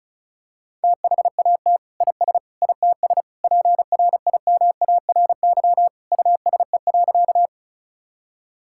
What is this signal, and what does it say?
Morse code, heard as a keyed tone, about 36 wpm, 700 Hz. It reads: THAT IS ITS PRIMARY USE.
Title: Sentences from Top 1000 Words - Review Part 5 - 35wpm